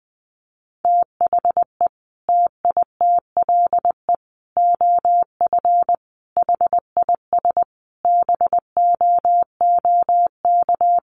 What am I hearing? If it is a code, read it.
Morse code, heard as a keyed tone, about 20 wpm, 700 Hz. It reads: THE TITLE OF HIS BOOK
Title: Sentences from Top 1200 Words 20wpm